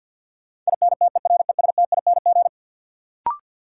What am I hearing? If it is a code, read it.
Morse code, heard as a keyed tone, about 53 wpm, 700 Hz. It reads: INTERESTING
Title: Sentences from Top 1000 Words - New Word Drill - 50wpm